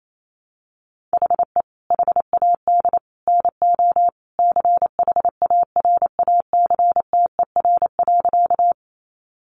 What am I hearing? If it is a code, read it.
Morse code, heard as a keyed tone, about 28 wpm, 700 Hz. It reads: HE HAD NO CHARACTER.
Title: Sentences from Top 1200 Words 28wpm